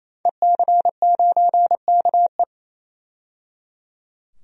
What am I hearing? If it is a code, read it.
Morse code, heard as a keyed tone, about 28 wpm, 700 Hz. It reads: EC9KE
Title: DXCC-100 Call Signs - 5 Characters - 28wpm